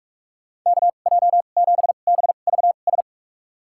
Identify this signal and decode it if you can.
Morse code, heard as a keyed tone, about 44 wpm, spaced as for 32 wpm, 700 Hz. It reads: KJ7BVS
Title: USA Call Signs - 2x Character Spacing - 45wpm